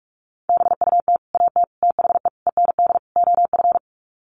Morse code, heard as a keyed tone, about 44 wpm, 700 Hz. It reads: BUT AT THE END OF
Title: Sets of 5 Words Encore2 45wpm